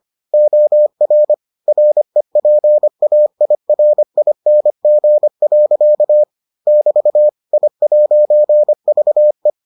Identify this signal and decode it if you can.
Morse code, heard as a keyed tone, about 25 wpm, 600 Hz. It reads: OR REPAIRING. <BT> I'VE